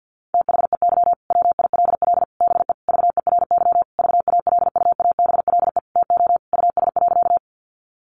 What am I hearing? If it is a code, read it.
Morse code, heard as a keyed tone, about 50 wpm, 700 Hz. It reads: THEY WILL BE VERY VALUABLE TO US.
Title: Sentences from Top 1100 Words 50wpm